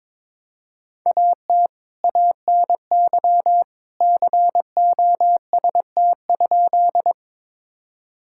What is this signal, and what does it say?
Morse code, heard as a keyed tone, about 22 wpm, 700 Hz. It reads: AT ANY COST?
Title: Sentences from Top 1000 Words - Review Part 5 - 22wpm